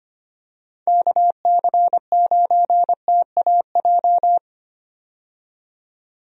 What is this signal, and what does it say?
Morse code, heard as a keyed tone, about 25 wpm, 700 Hz. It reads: KC9TAJ
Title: USA Call Signs - 6 Characters - 25wpm